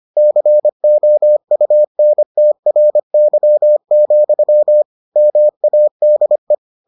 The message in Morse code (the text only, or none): COUNTRY, MADE